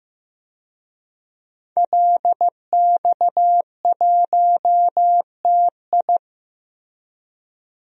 Morse code, heard as a keyed tone, about 15 wpm, 700 Hz. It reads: LX1TI
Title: DXCC-100 Call Signs - 5 Characters - Repeated 3x - 15wpm